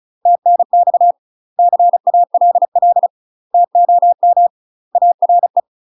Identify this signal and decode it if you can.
Morse code, heard as a keyed tone, about 33 wpm, 700 Hz. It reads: TNX CALL TOM ARE
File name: UK MASTER QSOs TRAINING / 35WPM / UK MASTER QSOs 0018